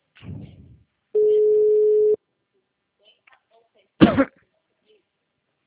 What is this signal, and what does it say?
Ringback tone (Europe, 425 Hz), with speech.